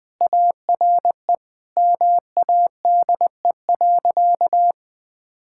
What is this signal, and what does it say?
Morse code, heard as a keyed tone, about 21 wpm, 700 Hz. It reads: ARE MADE.
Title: Sentences with 4-Word Phrases - Spoken Beginning - 20wpm